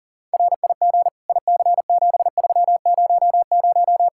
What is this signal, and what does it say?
Morse code, heard as a keyed tone, about 40 wpm, 700 Hz. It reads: RIG IC7300
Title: Practice QSO - Exchange 2 - Info - 40wpm